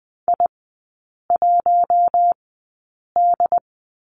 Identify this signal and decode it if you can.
Morse code, heard as a keyed tone, about 20 wpm, spaced as for 8 wpm, 700 Hz. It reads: I1D
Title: 138 Lesson 23e - Letter f - Three Characters 20wpm